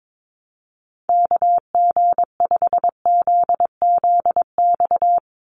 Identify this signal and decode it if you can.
Morse code, heard as a keyed tone, about 22 wpm, 700 Hz. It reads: KG5ZZX
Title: USA Call Signs 22wpm